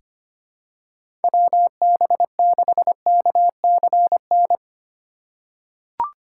Morse code, heard as a keyed tone, about 25 wpm, 700 Hz. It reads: WB6KCN
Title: USA Call Signs - 6 Characters - 6x Character Spacing - 25wpm